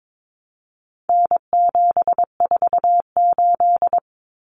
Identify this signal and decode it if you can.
Morse code, heard as a keyed tone, about 22 wpm, 700 Hz. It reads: N748
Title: Common Cut Numbers - 4-Digit 22wpm